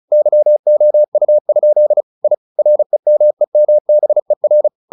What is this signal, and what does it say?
Morse code, heard as a keyed tone, about 34 wpm, 600 Hz. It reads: YOU? I REMEMBER
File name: the old man and the sea / the old man and the sea 1 35wpm